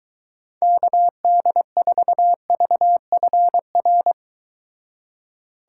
Morse code, heard as a keyed tone, about 23 wpm, 700 Hz. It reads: KD4VFR